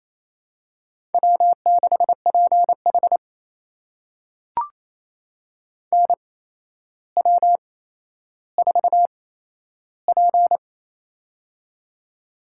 Morse code, heard as a keyed tone, about 28 wpm, spaced as for 12 wpm, 700 Hz. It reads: W6PH NW4P
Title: USA Call Signs - 4 Characters - 8x Character Spacing - 28wpm